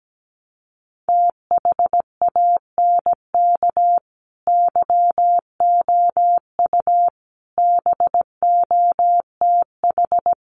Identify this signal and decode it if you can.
Morse code, heard as a keyed tone, about 17 wpm, 700 Hz. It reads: THANK YOU BOTH
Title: Sets of 5 Words 17wpm